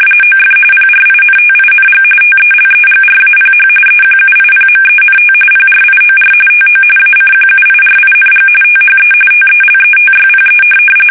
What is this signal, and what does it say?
RTTY, about 45.45 baud. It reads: OF PRINTING TELEGRAPH EQUIPMENT PRIOR TO ACQUIRING THE TELETYPE
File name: fsk850 / teletype-corp